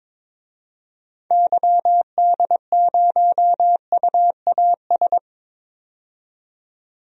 Morse code, heard as a keyed tone, about 22 wpm, 700 Hz. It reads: YD0UAS
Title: DXCC-100 Call Signs - 6 Characters - 22wpm